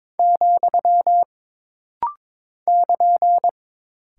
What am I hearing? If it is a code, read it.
Morse code, heard as a keyed tone, about 22 wpm, 700 Hz. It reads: , <KN>
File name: practice / 22wpm / Single Punctuation-Prosign - Rapid-Fire - 22wpm